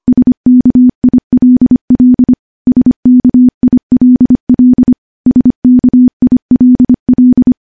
Morse code, heard as a keyed tone, about 25 wpm, 250 Hz. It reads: SKILL SKILL SKILL
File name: morse / morse-training-25WPM